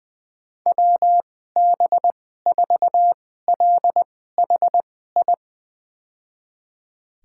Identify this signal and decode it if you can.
Morse code, heard as a keyed tone, about 21 wpm, spaced as for 15 wpm, 700 Hz. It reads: WB4LHI
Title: USA Call Signs - 6 Characters - 2x Character Spacing - 20wpm